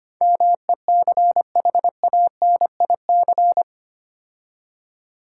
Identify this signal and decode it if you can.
Morse code, heard as a keyed tone, about 25 wpm, 690 Hz. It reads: MECHANIC
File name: 8 Letter Words - First 3 Letters Given - 25wpm